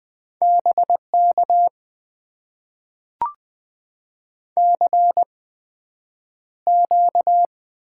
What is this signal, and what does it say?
Morse code, heard as a keyed tone, about 20 wpm, spaced as for 9 wpm, 700 Hz. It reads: BK CQ
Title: Top 90 QSO Elements - 8x Character Spacing - 20wpm